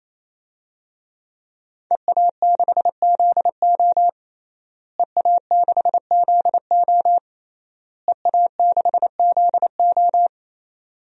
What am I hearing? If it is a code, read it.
Morse code, heard as a keyed tone, about 28 wpm, 700 Hz. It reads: EA6ZO EA6ZO EA6ZO
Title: DXCC-100 Call Signs - 5 Characters - Repeated 3x - 28wpm